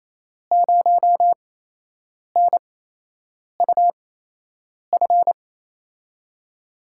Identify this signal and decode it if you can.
Morse code, heard as a keyed tone, about 28 wpm, spaced as for 8 wpm, 700 Hz. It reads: 0NUF
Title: USA Call Signs - 5 Characters - 8x Character Spacing - 28wpm